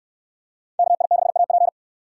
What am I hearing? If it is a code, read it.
Morse code, heard as a keyed tone, about 68 wpm, 690 Hz. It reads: KI6AP